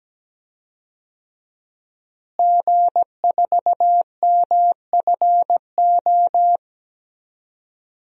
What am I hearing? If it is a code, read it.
Morse code, heard as a keyed tone, about 17 wpm, 700 Hz. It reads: G4MFO
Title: DXCC-100 Call Signs - 5 Characters - Repeated 2x - 17wpm